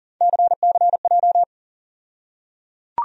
Morse code, heard as a keyed tone, about 40 wpm, 700 Hz. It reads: CCJ
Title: USA Call Sign Postfixes - 3 Letters 40wpm